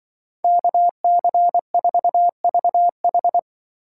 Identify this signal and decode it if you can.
Morse code, heard as a keyed tone, about 23 wpm, 700 Hz. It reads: KC4VH
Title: Speed Racing - USA 2x2 Call Signs 20wpm